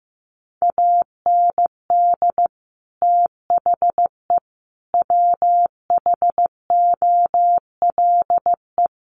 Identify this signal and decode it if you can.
Morse code, heard as a keyed tone, about 15 wpm, 700 Hz. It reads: AND THE WHOLE
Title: Sentences from Top 1300 Words 15wpm